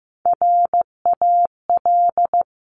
Morse code, heard as a keyed tone, about 15 wpm, 700 Hz. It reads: RAL
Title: Top Three Letter Combinations in Words 15wpm